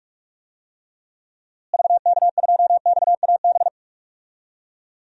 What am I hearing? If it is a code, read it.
Morse code, heard as a keyed tone, about 46 wpm, 700 Hz. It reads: VK2XAB
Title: DXCC-100 Call Signs - 6 Characters - Repeated 2x - 45wpm